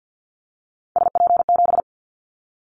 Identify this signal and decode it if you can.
Morse code, heard as a keyed tone, about 50 wpm, 700 Hz. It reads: SP7
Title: Callsign Prefixes for Top 25 DXCC - 50wpm